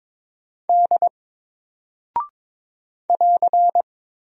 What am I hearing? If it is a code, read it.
Morse code, heard as a keyed tone, about 22 wpm, 700 Hz. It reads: D <AR>
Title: Single Character (Letter:Number:Punctuation:Prosign) 22wpm